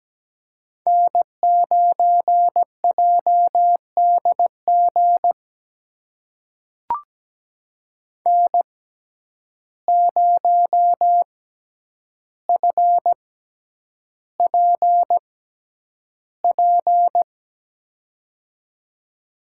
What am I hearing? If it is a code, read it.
Morse code, heard as a keyed tone, about 17 wpm, spaced as for 9 wpm, 700 Hz. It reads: N9JDG N0FPP
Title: USA Call Signs - 5 Characters - 6x Character Spacing - 17wpm